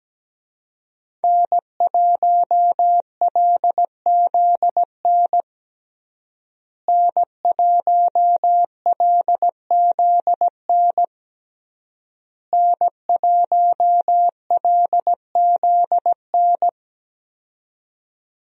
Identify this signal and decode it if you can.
Morse code, heard as a keyed tone, about 17 wpm, 700 Hz. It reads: N1LZN N1LZN N1LZN
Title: USA Call Signs - 5 Characters - Repeated 3x - 17wpm